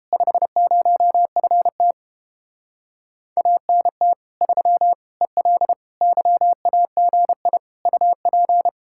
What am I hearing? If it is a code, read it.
Morse code, heard as a keyed tone, about 31 wpm, 700 Hz. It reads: 50FT ANT 3 EL YAGI UP